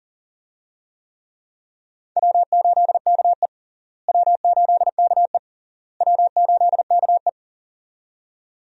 Morse code, heard as a keyed tone, about 40 wpm, 700 Hz. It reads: W8KE W8KE W8KE
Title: USA Call Signs - 4 Characters - Repeated 3x - 40wpm